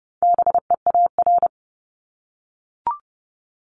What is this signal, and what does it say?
Morse code, heard as a keyed tone, about 30 wpm, 700 Hz. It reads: BEAR
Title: ING Suffix - Top 100 Words with Base Words 30wpm